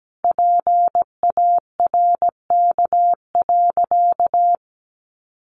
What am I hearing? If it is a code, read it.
Morse code, heard as a keyed tone, about 17 wpm, 700 Hz. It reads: PARK.
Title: Sentences from Top 1300 Words 17wpm